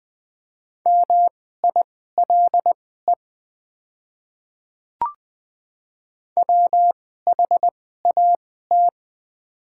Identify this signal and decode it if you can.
Morse code, heard as a keyed tone, about 20 wpm, spaced as for 14 wpm, 700 Hz. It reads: MILE WHAT
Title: 2x Character Spacing - Top 300 Words - 20wpm